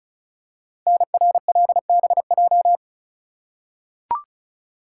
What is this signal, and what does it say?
Morse code, heard as a keyed tone, about 33 wpm, 700 Hz. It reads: NRLBJ
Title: Code Groups - 5 Letters - 35wpm